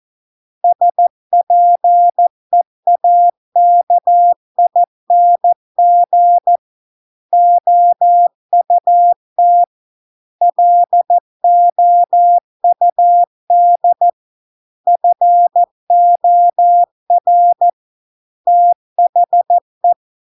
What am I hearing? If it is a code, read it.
Morse code, heard as a keyed tone, about 14 wpm, 700 Hz. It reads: SPEAKING OUT LOUD FOR THE